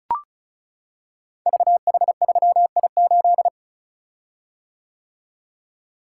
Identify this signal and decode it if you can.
Morse code, heard as a keyed tone, about 36 wpm, 700 Hz. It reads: VH3I8
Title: Code Groups - 5 Letter:Numbers - 35wpm